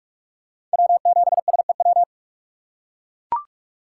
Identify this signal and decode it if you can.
Morse code, heard as a keyed tone, about 44 wpm, 700 Hz. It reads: W7SEW
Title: USA Call Signs - 5 Characters - 6x Character Spacing - 45wpm